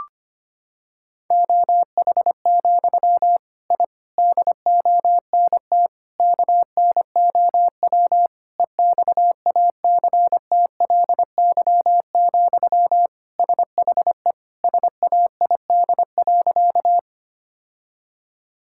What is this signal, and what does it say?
Morse code, heard as a keyed tone, about 25 wpm, 700 Hz. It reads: OH, I DONT KNOW EXACTLY, SHE SAID.
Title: Sentences from Top 1000 Words - Review Part 6 - 25wpm